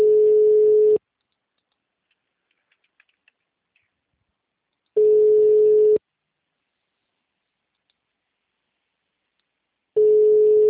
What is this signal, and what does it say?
Ringback tone (Europe, 425 Hz).
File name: out- -2011- - - .147